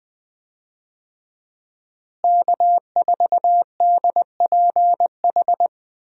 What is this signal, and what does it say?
Morse code, heard as a keyed tone, about 20 wpm, 700 Hz. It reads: K4DPH
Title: USA Call Signs - 5 Characters - Repeated 3x - 20wpm